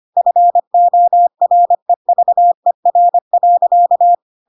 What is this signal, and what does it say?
Morse code, heard as a keyed tone, about 25 wpm, 700 Hz. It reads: FOREVER.
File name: Pinocchio / Pinocchio ch4 25 25 ik1hge